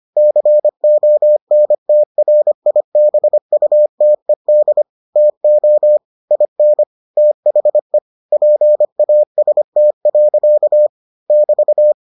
Morse code, heard as a keyed tone, about 25 wpm, 600 Hz. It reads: CONTRIBUTED TO IN THE PAST. <BT>